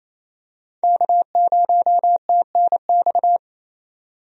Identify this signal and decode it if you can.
Morse code, heard as a keyed tone, about 28 wpm, 700 Hz. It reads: K0TNX
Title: USA Call Signs - 5 Characters - Repeated 3x - 28wpm